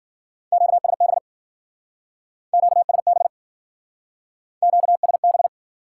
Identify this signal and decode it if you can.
Morse code, heard as a keyed tone, about 52 wpm, 700 Hz. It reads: QSB QSB QSB